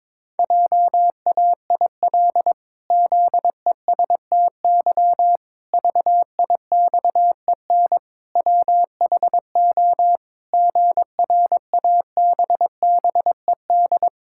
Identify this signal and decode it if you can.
Morse code, heard as a keyed tone, about 22 wpm, 700 Hz. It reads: JAIL ZESTY VIXEN WHO GRABBED
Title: Pangram Challenge v2 - 22wpm